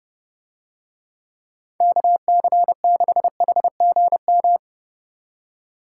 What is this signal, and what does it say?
Morse code, heard as a keyed tone, about 31 wpm, 700 Hz. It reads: KC6HGM